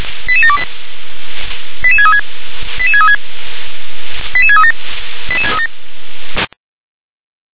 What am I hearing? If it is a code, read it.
Five-tone selective call (ZVEI), heard as quick runs of five tones, tones of about 70 ms. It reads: 88728 88636 88636 88637 88637